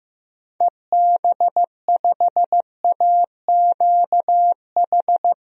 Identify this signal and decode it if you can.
Morse code, heard as a keyed tone, about 15 wpm, 700 Hz. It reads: EB5AQH
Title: DXCC-100 Call Signs - 6 Characters - Repeated 3x - 15wpm